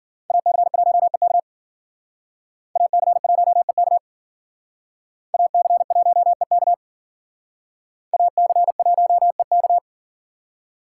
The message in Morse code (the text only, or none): AC1EK AC1EK AC1EK AC1EK